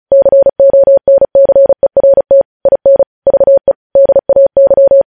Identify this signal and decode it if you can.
Morse code, heard as a keyed tone, about 36 wpm, 550 Hz. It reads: CONCERT IN VE DAY
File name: podcast / News headlines-1x-35x35